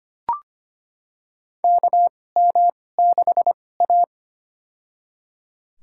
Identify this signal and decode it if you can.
Morse code, heard as a keyed tone, about 25 wpm, spaced as for 18 wpm, 700 Hz. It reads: KM6A
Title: USA Call Signs - 4 Characters - 2x Character Spacing - 25wpm